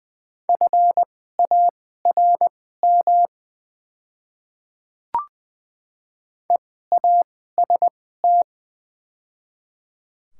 Morse code, heard as a keyed tone, about 20 wpm, spaced as for 15 wpm, 690 Hz. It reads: FARM EAST
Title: 2x Character Spacing - Top 500 Words - 20wpm